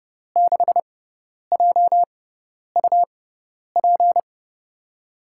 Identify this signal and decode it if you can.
Morse code, heard as a keyed tone, about 30 wpm, spaced as for 10 wpm, 700 Hz. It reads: 6JUP
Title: USA Call Signs - 6 Characters - 6x Character Spacing - 30wpm